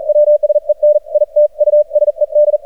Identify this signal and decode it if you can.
Morse code, heard as a keyed tone, about 43 wpm, 600 Hz. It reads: WSEN IT USED
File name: archived / CRSnet